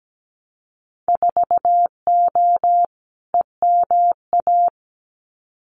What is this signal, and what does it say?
Morse code, heard as a keyed tone, about 17 wpm, 700 Hz. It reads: 4O EMA
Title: Winter Field Day Exchanges 17wpm